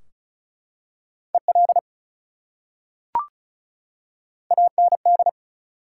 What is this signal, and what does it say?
Morse code, heard as a keyed tone, about 34 wpm, 700 Hz. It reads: EL AND
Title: Top 180 QSO Elements 35wpm